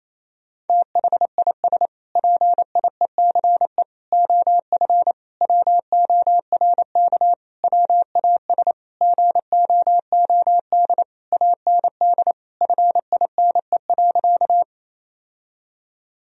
Morse code, heard as a keyed tone, about 27 wpm, 700 Hz. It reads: THIS PIECE OF WORK WAS GOOD AND FINE.
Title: Sentences from Top 1000 Words - Review Part 2 - 28wpm